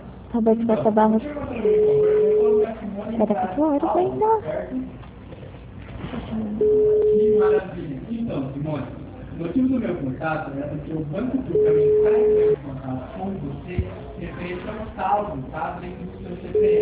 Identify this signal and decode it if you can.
Ringback tone (Europe, 425 Hz), with speech.